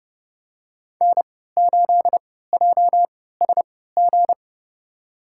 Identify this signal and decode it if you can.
Morse code, heard as a keyed tone, about 31 wpm, spaced as for 17 wpm, 700 Hz. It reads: N8JSG
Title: USA Call Signs - 5 Characters - 3x Character Spacing - 30wpm